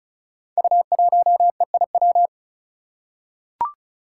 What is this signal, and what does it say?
Morse code, heard as a keyed tone, about 36 wpm, 700 Hz. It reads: U1EIW
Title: Five Challenging Letter-Number Combinations 35wpm